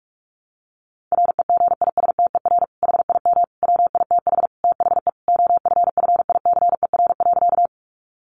Morse code, heard as a keyed tone, about 44 wpm, 700 Hz. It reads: REGISTER HIM WITH THE OFFICER.
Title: Sentences from Top 1200 Words 45wpm